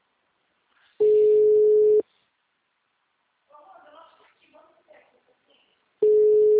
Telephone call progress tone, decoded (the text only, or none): ringback tone